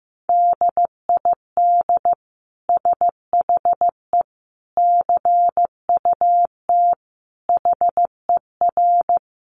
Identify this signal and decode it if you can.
Morse code, heard as a keyed tone, about 15 wpm, 700 Hz. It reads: DID SHE CUT HER
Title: Sentences from Top 1300 Words 15wpm